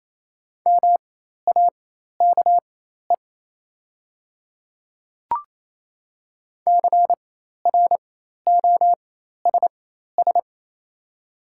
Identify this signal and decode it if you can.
Morse code, heard as a keyed tone, about 28 wpm, spaced as for 13 wpm, 700 Hz. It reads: MAKE CROSS
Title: 4x Character Spacing - Top 300 Words - 28wpm